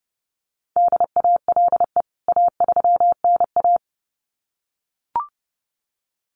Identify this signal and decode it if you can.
Morse code, heard as a keyed tone, about 30 wpm, 700 Hz. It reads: DALE A3NA
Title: CWT Exchange 30wpm